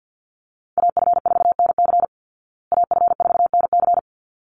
Morse code, heard as a keyed tone, about 48 wpm, 700 Hz. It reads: AF4NC AF4NC
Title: Call Signs Repeated 3x - 50wpm